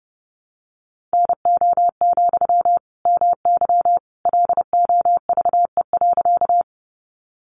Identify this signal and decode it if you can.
Morse code, heard as a keyed tone, about 30 wpm, 700 Hz. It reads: NO, MY LOVE.
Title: Sentences from Top 400 Words 30wpm